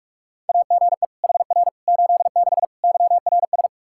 Morse code, heard as a keyed tone, about 42 wpm, 700 Hz. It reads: AGE HR 86 YRS
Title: Practice QSO - Exchange 3 - More Info - 45wpm